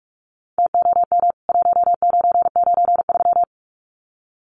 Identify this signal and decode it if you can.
Morse code, heard as a keyed tone, about 44 wpm, 700 Hz. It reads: TOM 1993
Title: CWT Exchange 45wpm